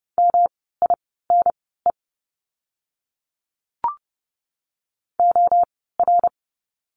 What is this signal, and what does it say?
Morse code, heard as a keyed tone, about 30 wpm, spaced as for 17 wpm, 700 Hz. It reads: MINE OR
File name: course / 30:10wpm / 108 Lesson 18f - Letter m - Words 30wpm